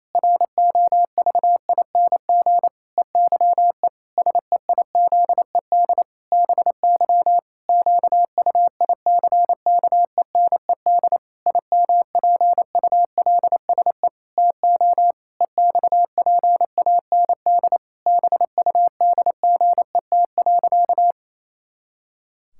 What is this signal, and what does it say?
Morse code, heard as a keyed tone, about 28 wpm, 700 Hz. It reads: ROVING EYE SEIZED BY QUICKENED IMPULSE TO EXPAND BUDGET.
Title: Pangram Challenge v2 - 28wpm